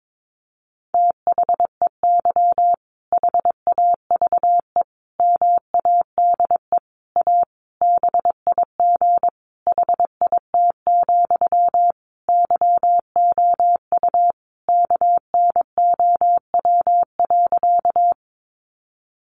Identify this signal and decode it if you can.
Morse code, heard as a keyed tone, about 22 wpm, 700 Hz. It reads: THEY HAVE MADE A BIG HIT, YOU KNOW.
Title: Sentences from Top 1200 Words 22wpm